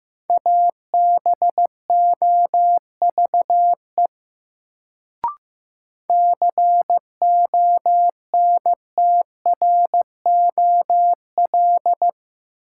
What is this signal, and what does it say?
Morse code, heard as a keyed tone, about 15 wpm, 700 Hz. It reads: ABOVE CONTROL
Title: Sentences from Top 1000 Words - Review Part 2 - New Word Drill - 15wpm